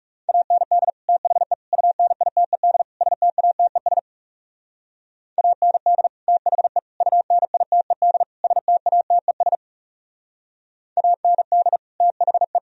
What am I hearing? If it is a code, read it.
Morse code, heard as a keyed tone, about 39 wpm, 700 Hz. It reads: AND THE UNITED STATES AND THE UNITED STATES AND THE